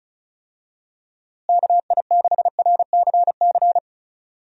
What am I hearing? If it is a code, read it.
Morse code, heard as a keyed tone, about 34 wpm, 700 Hz. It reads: KI6RCC
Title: USA Call Signs - 6 Characters - Repeated 2x - 35wpm